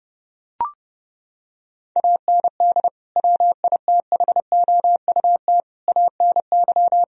Morse code, heard as a keyed tone, about 32 wpm, 700 Hz. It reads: AND WITHOUT ANY
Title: Sentences from Top 1000 Words - Review Part 4 - 30wpm